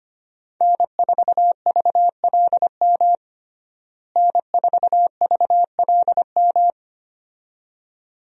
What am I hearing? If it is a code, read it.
Morse code, heard as a keyed tone, about 25 wpm, 700 Hz. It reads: N4VLM N4VLM
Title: USA Call Signs - 5 Characters - Repeated 2x - 25wpm